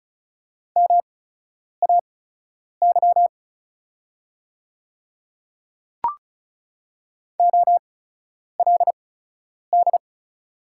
Morse code, heard as a keyed tone, about 36 wpm, spaced as for 10 wpm, 700 Hz. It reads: MAY OLD
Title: 8x Character Spacing - Top 300 Words - 35wpm